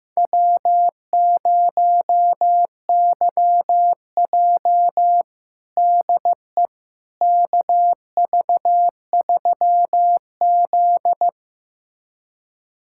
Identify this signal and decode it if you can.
Morse code, heard as a keyed tone, about 15 wpm, 700 Hz. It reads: W0YJ DE KV3Z
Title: USA Callsign DE Callsign - 4 Characters - 15wpm